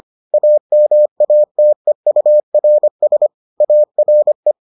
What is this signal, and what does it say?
Morse code, heard as a keyed tone, about 25 wpm, 600 Hz. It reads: AMATEURS ARE